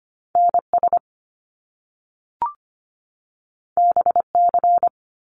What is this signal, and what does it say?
Morse code, heard as a keyed tone, about 25 wpm, 700 Hz. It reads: NS BC
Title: Canadian Provinces and Territories 25wpm